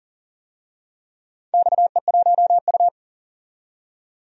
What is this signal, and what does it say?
Morse code, heard as a keyed tone, about 40 wpm, 690 Hz. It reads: XE1U
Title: DXCC-100 Call Signs - 4 Characters - 40wpm